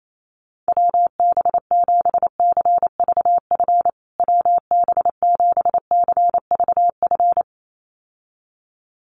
Morse code, heard as a keyed tone, about 28 wpm, 700 Hz. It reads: WB7CVF WB7CVF
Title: USA Call Signs Repeated 2x 28wpm